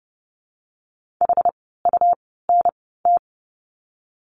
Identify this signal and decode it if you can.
Morse code, heard as a keyed tone, about 30 wpm, spaced as for 17 wpm, 700 Hz. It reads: HUNT
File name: Lesson 17f - Letter u - Words 30wpm